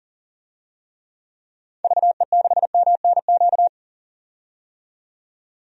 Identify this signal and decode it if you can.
Morse code, heard as a keyed tone, about 40 wpm, 700 Hz. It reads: VE6MNQ
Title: DXCC-100 Call Signs - 6 Characters - 40wpm